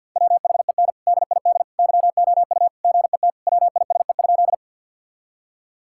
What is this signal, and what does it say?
Morse code, heard as a keyed tone, about 50 wpm, 700 Hz. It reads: WHEN DID YOU GET WISE?
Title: Sentences from Top 1600 Words 50wpm